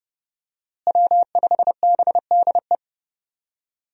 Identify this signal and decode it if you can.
Morse code, heard as a keyed tone, about 30 wpm, 700 Hz. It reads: W5BDE
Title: USA Call Signs - 5 Characters - Repeated 2x - 30wpm